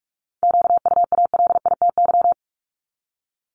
Morse code, heard as a keyed tone, about 46 wpm, 700 Hz. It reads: QUALITY
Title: Multi-letter Phonemes Repeated 3x 45wpm